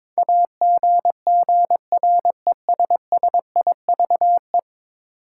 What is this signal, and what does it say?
Morse code, heard as a keyed tone, about 22 wpm, 700 Hz. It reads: AGGRESSIVE
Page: morse-code-ninja-practice-22wpm